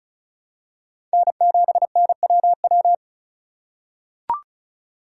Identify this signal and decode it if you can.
Morse code, heard as a keyed tone, about 36 wpm, 700 Hz. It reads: N7NWW
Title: USA Call Signs - 5 Characters - 4x Character Spacing - 35wpm